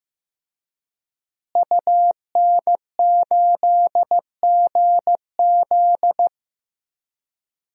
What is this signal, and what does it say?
Morse code, heard as a keyed tone, about 15 wpm, 700 Hz. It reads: UN8GZ
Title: DXCC-100 Call Signs - 5 Characters - Repeated 3x - 15wpm